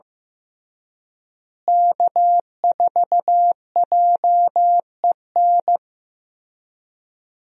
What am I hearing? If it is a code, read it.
Morse code, heard as a keyed tone, about 15 wpm, 700 Hz. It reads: K4JEN
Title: USA Call Signs - 5 Characters - Repeated 3x - 15wpm